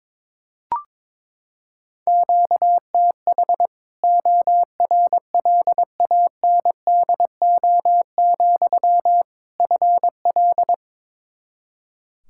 Morse code, heard as a keyed tone, about 22 wpm, 700 Hz. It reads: QTH ORLANDO, FL
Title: Practice QSO - Exchange 1 - Beginning - 22wpm